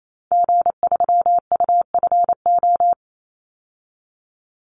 Morse code, heard as a keyed tone, about 28 wpm, 700 Hz. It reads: G3UFO
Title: Call Signs - 28wpm